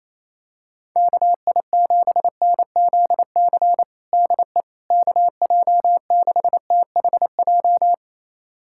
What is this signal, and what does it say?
Morse code, heard as a keyed tone, about 28 wpm, 700 Hz. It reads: KI7NZC DE KJ6THJ
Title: USA Callsign DE Callsign - 6 Characters - 28wpm